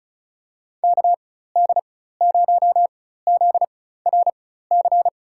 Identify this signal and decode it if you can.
Morse code, heard as a keyed tone, about 34 wpm, spaced as for 16 wpm, 700 Hz. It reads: KD0ZRC
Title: USA Call Signs - 6 Characters - 4x Character Spacing - 35wpm